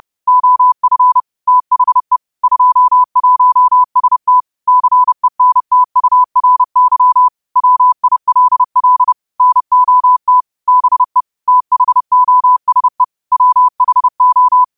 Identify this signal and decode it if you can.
Morse code, heard as a keyed tone, about 30 wpm, 1000 Hz. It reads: OF THE 21ST CENTURY WILL NOT BE THOSE WHO
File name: qotdfiles / 30wpm